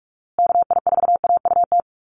Morse code, heard as a keyed tone, about 44 wpm, 700 Hz. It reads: KI4AUT
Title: USA Call Signs Repeated 2x 45wpm